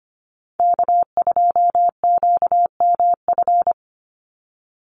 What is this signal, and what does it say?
Morse code, heard as a keyed tone, about 25 wpm, 700 Hz. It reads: K2QMF
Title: Call Signs Repeated 2x - 25wpm